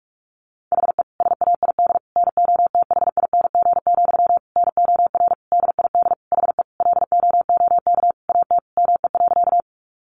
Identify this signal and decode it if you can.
Morse code, heard as a keyed tone, about 44 wpm, 700 Hz. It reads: HE SAID NOTHING, NOR DID HE LOOK AT ME.
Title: Sentences from Top 1500 Words 45wpm